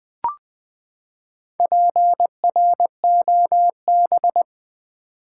Morse code, heard as a keyed tone, about 21 wpm, 700 Hz. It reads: PROB
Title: Build A Word - 4 Letters At A Time - 11-Letter Words - 20wpm